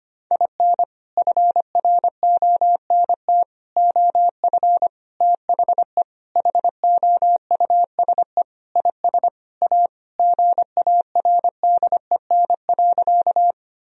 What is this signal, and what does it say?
Morse code, heard as a keyed tone, about 25 wpm, 700 Hz. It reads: IN FRONT OF THE HOUSE IS A GARDEN.
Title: Sentences from Top 1000 Words - Review Part 4 - 25wpm